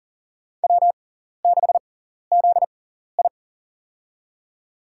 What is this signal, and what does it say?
Morse code, heard as a keyed tone, about 40 wpm, spaced as for 14 wpm, 700 Hz. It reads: W6ZI
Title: USA Call Signs - 4 Characters - 6x Character Spacing - 40wpm